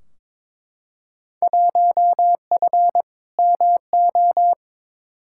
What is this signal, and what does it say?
Morse code, heard as a keyed tone, about 22 wpm, 700 Hz. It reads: 1F MO
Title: Field Day Exchanges Encore - 22wpm